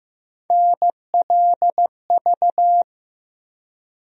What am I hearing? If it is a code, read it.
Morse code, heard as a keyed tone, about 15 wpm, 700 Hz. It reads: NLV
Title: USA Call Sign Postfixes - 3 Letters 15wpm